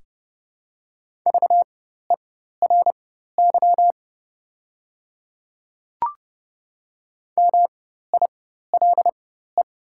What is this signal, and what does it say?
Morse code, heard as a keyed tone, about 30 wpm, spaced as for 14 wpm, 700 Hz. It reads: VERY MILE
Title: 4x Character Spacing - Top 500 Words - 30wpm